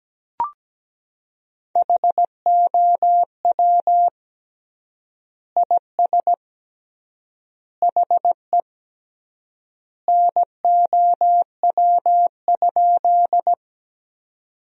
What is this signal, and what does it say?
Morse code, heard as a keyed tone, about 17 wpm, 700 Hz. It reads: HOW IS HE NOW?
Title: 3x Word-Spacing - Sentences from Top 100 Words - 17wpm